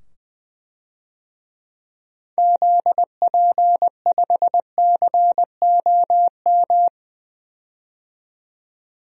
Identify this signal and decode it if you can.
Morse code, heard as a keyed tone, about 21 wpm, 700 Hz. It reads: ZP5COM
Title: DXCC-100 Call Signs - 6 Characters - 20wpm